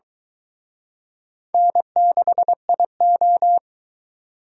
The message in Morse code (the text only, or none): N6IO